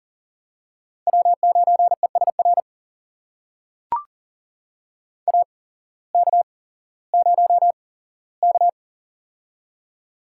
Morse code, heard as a keyed tone, about 38 wpm, spaced as for 18 wpm, 700 Hz. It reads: W9ESR AK0K
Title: USA Call Signs - 8x Character Spacing - 40wpm